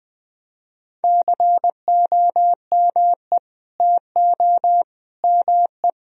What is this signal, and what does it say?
Morse code, heard as a keyed tone, about 21 wpm, 700 Hz. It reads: COME TO ME
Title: Sentences from Top 1000 Words - Review Part 4 - 20wpm